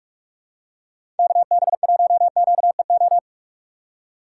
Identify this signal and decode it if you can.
Morse code, heard as a keyed tone, about 42 wpm, 690 Hz. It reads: KB1QEO
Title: USA Call Signs - 6 Characters - 2x Character Spacing - 45wpm